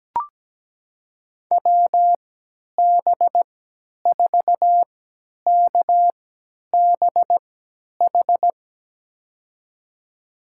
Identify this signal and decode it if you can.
Morse code, heard as a keyed tone, about 17 wpm, spaced as for 10 wpm, 700 Hz. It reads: WB4KBH
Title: USA Call Signs - 3x Character Spacing - 17wpm